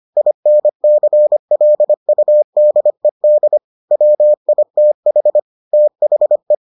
Morse code, heard as a keyed tone, about 25 wpm, 600 Hz. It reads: INCLUDED WITH THE